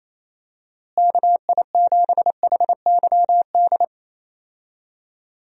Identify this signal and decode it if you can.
Morse code, heard as a keyed tone, about 28 wpm, 700 Hz. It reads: KI7HYD
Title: USA Call Signs - 6 Characters - Repeated 3x - 28wpm